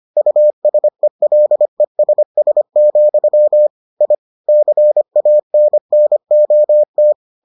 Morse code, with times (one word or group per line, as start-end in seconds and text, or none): 0.2-3.7 USELESS,
4.0-4.1 I
4.5-7.1 CANNOT